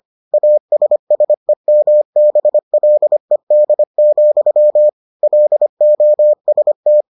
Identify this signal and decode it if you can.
Morse code, heard as a keyed tone, about 25 wpm, 600 Hz. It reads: ASSEMBLED, LOST